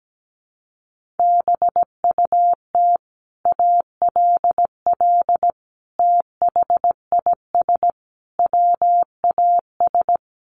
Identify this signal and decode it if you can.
Morse code, heard as a keyed tone, about 17 wpm, 700 Hz. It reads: BUT ALL THIS WAS
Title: Sentences from Top 1400 Words 17wpm